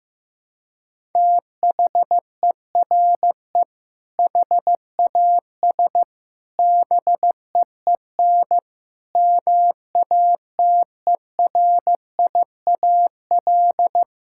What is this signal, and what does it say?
Morse code, heard as a keyed tone, about 15 wpm, 700 Hz. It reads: THERE HAS BEEN MATERIAL
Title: Sentences from Top 1000 Words - Review Part 5 - 15wpm